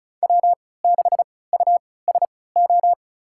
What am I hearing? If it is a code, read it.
Morse code, heard as a keyed tone, about 36 wpm, spaced as for 20 wpm, 700 Hz. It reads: W6USO
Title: USA Call Signs - 5 Characters - 3x Character Spacing - 35wpm